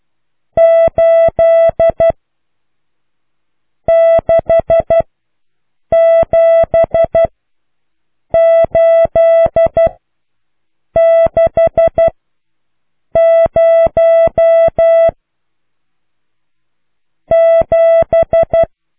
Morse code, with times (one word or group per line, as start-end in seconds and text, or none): 0.6-2.1 8
3.9-5.0 6
5.9-7.2 7
8.3-9.9 8
11.0-12.1 6
13.2-15.1 0
17.3-18.6 7